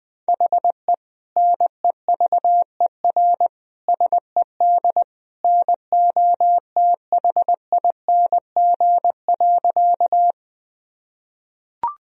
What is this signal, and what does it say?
Morse code, heard as a keyed tone, about 19 wpm, 700 Hz. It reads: HE NEVER SED NOTHING.
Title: Sentences from Top 500 Words and Abbreviations 20wpm